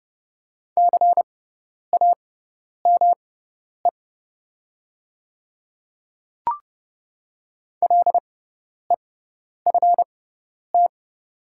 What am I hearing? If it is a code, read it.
Morse code, heard as a keyed tone, about 30 wpm, spaced as for 10 wpm, 690 Hz. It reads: CAME LEFT
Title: 6x Character Spacing - Top 500 Words - 30wpm